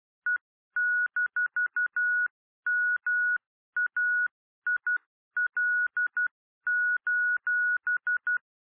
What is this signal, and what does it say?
Morse code, heard as a keyed tone, about 12 wpm, 1460 Hz. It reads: E-MAIL:
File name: radiocap / REC z